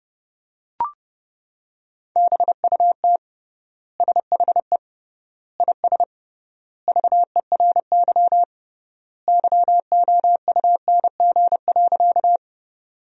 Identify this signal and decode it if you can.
Morse code, heard as a keyed tone, about 32 wpm, 700 Hz. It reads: BUT SHE IS VERY YOUNG.
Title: 3x Word-Spacing - Sentences from Top 400 Words - 30wpm